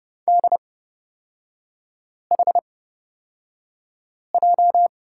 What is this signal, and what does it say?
Morse code, heard as a keyed tone, about 29 wpm, spaced as for 5 wpm, 700 Hz. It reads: DHJ
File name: Lesson 37e - Letter x - Three Characters 30wpm